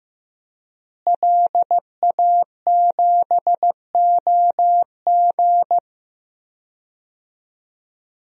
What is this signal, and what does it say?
Morse code, heard as a keyed tone, about 15 wpm, 700 Hz. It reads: LA7OG
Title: DXCC-100 Call Signs - 5 Characters - Repeated 3x - 15wpm